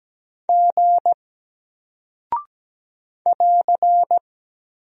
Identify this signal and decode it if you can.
Morse code, heard as a keyed tone, about 17 wpm, 700 Hz. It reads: G <AR>
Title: Single Character (Letter:Number:Punctuation:Prosign) Alt 17wpm